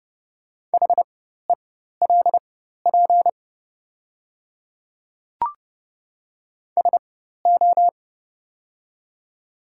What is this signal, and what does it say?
Morse code, heard as a keyed tone, about 31 wpm, spaced as for 14 wpm, 700 Hz. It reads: HELP SO
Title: 4x Character Spacing - Top 200 Words - 30wpm